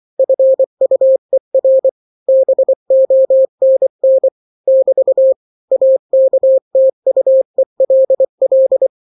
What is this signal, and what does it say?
Morse code, heard as a keyed tone, about 24 wpm, 530 Hz. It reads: FUER BONN <BT> AKTUELL
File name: daily / 23wpm